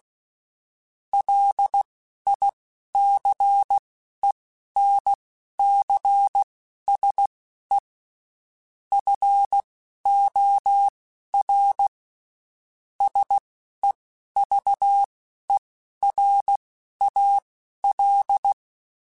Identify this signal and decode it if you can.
Morse code, heard as a keyed tone, about 16 wpm, spaced as for 12 wpm, 800 Hz. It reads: LICENCSE FOR SEVERAL
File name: (spaced) / QSOs Spaced-mp3s / kc7sac @15 sp